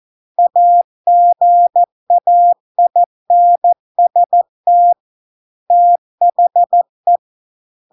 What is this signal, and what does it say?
Morse code, heard as a keyed tone, about 14 wpm, 700 Hz. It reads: AGAINST THE